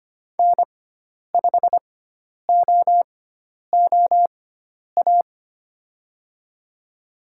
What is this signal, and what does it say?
Morse code, heard as a keyed tone, about 25 wpm, spaced as for 10 wpm, 700 Hz. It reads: N5OOA
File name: Lesson 16g - Letter c - Call Signs 25wpm